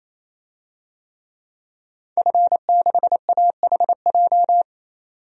Morse code, heard as a keyed tone, about 28 wpm, 700 Hz. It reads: F6AHJ
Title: DXCC-100 Call Signs - 5 Characters - 28wpm